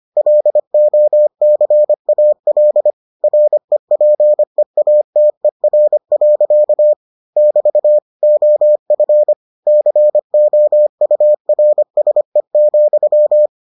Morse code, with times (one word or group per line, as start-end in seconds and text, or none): 0.2-2.9 LOCAL
3.2-6.9 REPEATER.
7.4-8.0 <BT>
8.2-9.3 OF
9.7-13.5 COURSE,